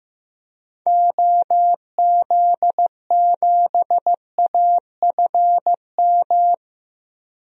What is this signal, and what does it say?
Morse code, heard as a keyed tone, about 15 wpm, 700 Hz. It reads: OZ7AFM
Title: DXCC-100 Call Signs - 6 Characters - Repeated 3x - 15wpm